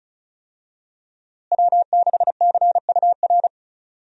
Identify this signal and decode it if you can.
Morse code, heard as a keyed tone, about 34 wpm, 700 Hz. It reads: W6CUR
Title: USA Call Signs - 5 Characters - Repeated 2x - 35wpm